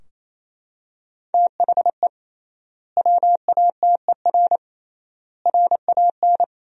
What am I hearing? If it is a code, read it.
Morse code, heard as a keyed tone, about 28 wpm, 700 Hz. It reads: THE WATER RAN
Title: 3x Word-Spacing - Sentences from Top 500 Words - 28wpm